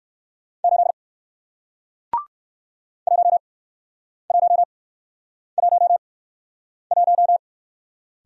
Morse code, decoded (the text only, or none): 8 1111